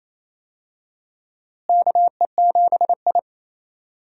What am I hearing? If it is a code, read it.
Morse code, heard as a keyed tone, about 28 wpm, 700 Hz. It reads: KE7I
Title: USA Call Signs - 4 Characters - Repeated 2x - 28wpm